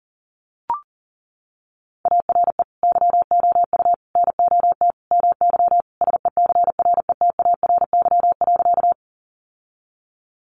Morse code, heard as a keyed tone, about 40 wpm, 700 Hz. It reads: ARE YOU NOT MY SECRETARY.
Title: Sentences from Top 1300 Words 40wpm